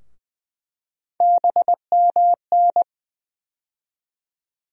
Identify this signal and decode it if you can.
Morse code, heard as a keyed tone, about 20 wpm, 700 Hz. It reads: BMN
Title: USA Call Sign Postfixes - 3 Letters 20wpm